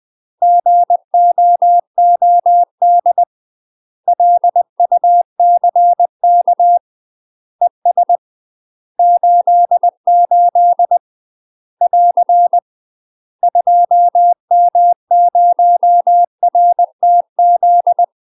Morse code, heard as a keyed tone, about 20 wpm, 700 Hz. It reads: GOOD LUCK ES 88 <AR> 2M0RTZ
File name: UK MASTER QSOs TRAINING / 20WPM / UK MASTER QSOs 0003